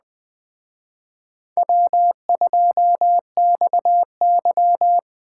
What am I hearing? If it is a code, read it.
Morse code, heard as a keyed tone, about 20 wpm, 700 Hz. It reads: W2XY